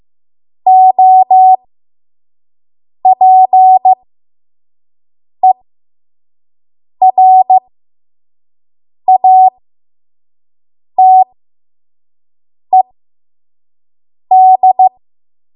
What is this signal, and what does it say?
Morse code, heard as a keyed tone, about 15 wpm, spaced as for 5 wpm, 750 Hz. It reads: OPERATED
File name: Morse / 05WPM